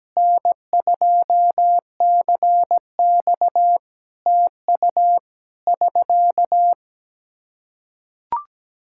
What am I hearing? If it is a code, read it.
Morse code, heard as a keyed tone, about 17 wpm, 700 Hz. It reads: N2CX TU <SK>
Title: K1USN SST QSO Practice 17wpm